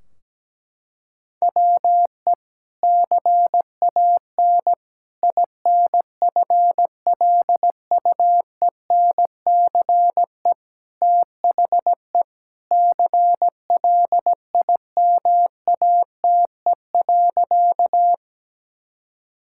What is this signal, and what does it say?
Morse code, heard as a keyed tone, about 17 wpm, 700 Hz. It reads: WE CAN INFLUENCE THE CLIMATE.
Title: Sentences from Top 1600 Words 17wpm